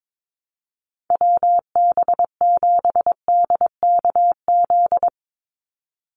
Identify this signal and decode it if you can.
Morse code, heard as a keyed tone, about 22 wpm, 700 Hz. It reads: WB7DKZ
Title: Call Signs Repeated 3x - 22wpm